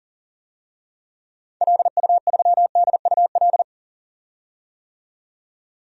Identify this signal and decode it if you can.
Morse code, heard as a keyed tone, about 40 wpm, 700 Hz. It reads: LU3DUL
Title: DXCC-100 Call Signs - 6 Characters - 40wpm